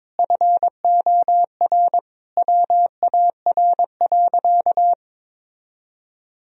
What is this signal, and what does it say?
Morse code, heard as a keyed tone, about 22 wpm, 700 Hz. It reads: FOR WAR.
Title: Sentences from Top 1000 Words - Review Part 3 - 22wpm